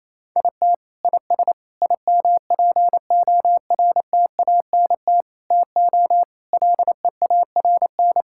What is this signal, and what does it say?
Morse code, heard as a keyed tone, about 28 wpm, 700 Hz. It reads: IT IS IMPORTANT TO LEARN
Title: Sentences from Top 1000 Words - Review Part 6 - 28wpm